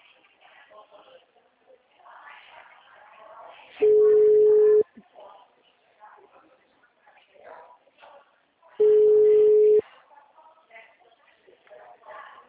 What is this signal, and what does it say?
Ringback tone (Europe, 425 Hz), with speech.
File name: out- -2022- - - .34488